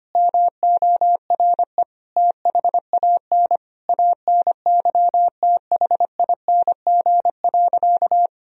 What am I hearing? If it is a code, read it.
Morse code, heard as a keyed tone, about 25 wpm, 700 Hz. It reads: MORE THAN ANYTHING.
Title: Sentences from Top 1000 Words - Review Part 3 - 25wpm